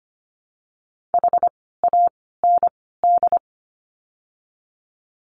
Morse code, heard as a keyed tone, about 25 wpm, 700 Hz. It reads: H A N D
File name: course / 25:10wpm / 094 Lesson 16f - Letter c - Words 25wpm